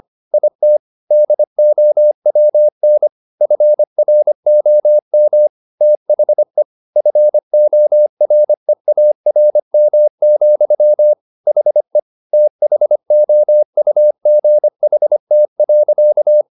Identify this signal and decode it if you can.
Morse code, heard as a keyed tone, about 25 wpm, 600 Hz. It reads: IT DOWN FROM THE FOREARM, HE THOUGHT.
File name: the old man and the sea part 1 25wpm